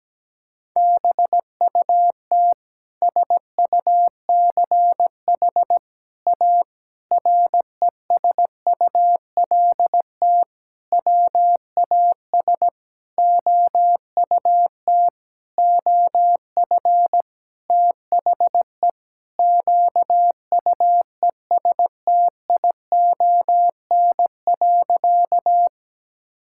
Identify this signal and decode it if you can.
Morse code, heard as a keyed tone, about 17 wpm, 700 Hz. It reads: BUT SUCH A RESULT WAS OUT OF THE QUESTION.
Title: Sentences from Top 1000 Words - Review Part 5 - 17wpm